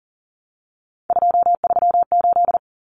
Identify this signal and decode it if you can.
Morse code, heard as a keyed tone, about 38 wpm, 700 Hz. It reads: 238
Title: Numbers - 3-Digit 40wpm